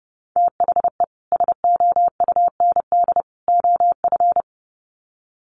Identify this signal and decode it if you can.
Morse code, heard as a keyed tone, about 30 wpm, 700 Hz. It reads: THE SOUND OF
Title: Sets of 3 Words 30wpm